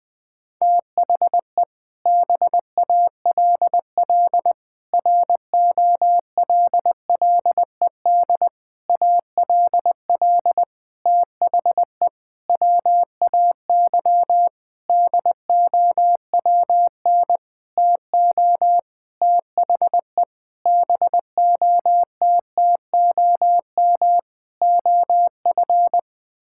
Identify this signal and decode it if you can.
Morse code, heard as a keyed tone, about 20 wpm, 700 Hz. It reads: THE BALL ROLLED ALL THE WAY DOWN TO THE BOTTOM OF
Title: Sentences with 6-Word Phrases - Spoken Beginning - 20wpm